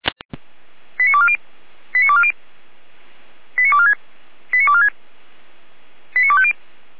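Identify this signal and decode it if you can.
Five-tone selective call (ZVEI), heard as quick runs of five tones, tones of about 70 ms. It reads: 89240 89240 89246 89246 89250
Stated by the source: ZVEI five-tone selective call 89230, received by SDR and served by Feuerwehr Orlamünde.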